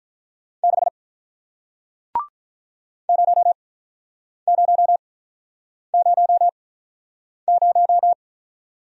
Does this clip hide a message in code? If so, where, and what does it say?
Yes: Morse code from 0.6 to 8.1 s, heard as a keyed tone, about 47 wpm, spaced as for 9 wpm, 700 Hz. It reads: 6 0000